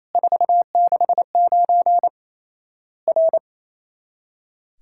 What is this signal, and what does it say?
Morse code, heard as a keyed tone, about 28 wpm, 700 Hz. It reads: 469 R
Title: Worked All Europe DX Contest - QTC Exchange - 28wpm